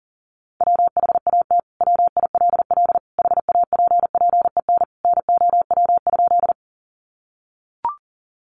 Morse code, heard as a keyed tone, about 40 wpm, 700 Hz. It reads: WHAT WILL HAPPEN NOW?
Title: Sentences from Top 400 Words 40wpm